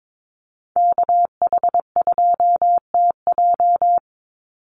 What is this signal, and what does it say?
Morse code, heard as a keyed tone, about 22 wpm, 700 Hz. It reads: KH2TJ
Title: NA SOTA Chaser Callsigns 22wpm